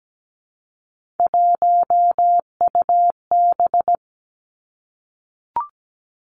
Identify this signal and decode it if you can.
Morse code, heard as a keyed tone, about 17 wpm, 700 Hz. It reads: 1UB
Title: Three Challenging Letter-Number Combinations 17wpm